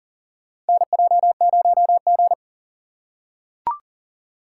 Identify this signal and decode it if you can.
Morse code, heard as a keyed tone, about 40 wpm, 700 Hz. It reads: NJ0G
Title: USA Call Signs - 4 Characters - 2x Character Spacing - 40wpm